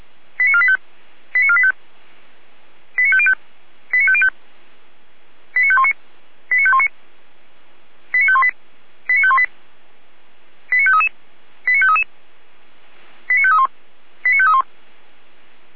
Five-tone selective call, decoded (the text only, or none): 89485 89485 89505 89505 89519 89519 89518 89518 89533 89533 89531 89531